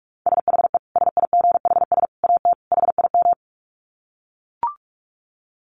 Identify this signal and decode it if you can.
Morse code, heard as a keyed tone, about 44 wpm, 700 Hz. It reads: SHE SIGHS AT HIM
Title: Dit Challenge 45wpm